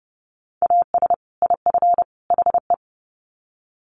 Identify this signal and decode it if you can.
Morse code, heard as a keyed tone, about 30 wpm, 700 Hz. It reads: AS IF HE
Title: Sets of 3 Words 30wpm